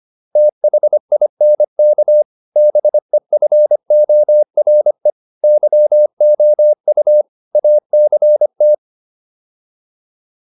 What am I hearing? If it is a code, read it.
Morse code, heard as a keyed tone, about 25 wpm, 600 Hz. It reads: THINK BEFORE YOU ACT